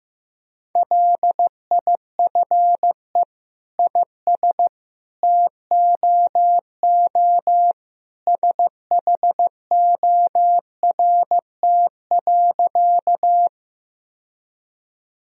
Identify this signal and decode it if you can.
Morse code, heard as a keyed tone, about 15 wpm, 700 Hz. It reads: LIFE IS TOO SHORT.
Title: Sentences from Top 1000 Words - Review Part 2 - 15wpm